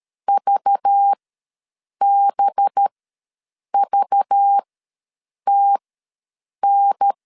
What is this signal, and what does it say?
Morse code, heard as a keyed tone, about 13 wpm, spaced as for 7 wpm, 800 Hz. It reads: VBVTN
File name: LEZIONI / 5 LBVC / 5 LBV ws 12wpm 3015